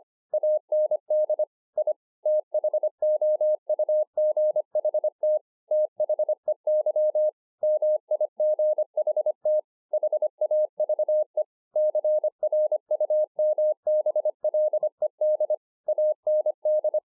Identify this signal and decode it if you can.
Morse code, heard as a keyed tone, about 25 wpm, 610 Hz. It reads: AND I THOUGHT THEY MIGHT HAVE CRUMBLED AND